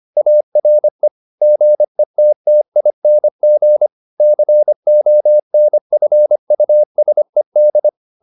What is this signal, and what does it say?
Morse code, heard as a keyed tone, about 25 wpm, 600 Hz. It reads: ARE GETTING CONFUSED